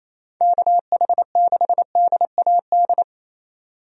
Morse code, heard as a keyed tone, about 28 wpm, 700 Hz. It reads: KH6DAD
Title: DXCC-100 Call Signs - 6 Characters - 28wpm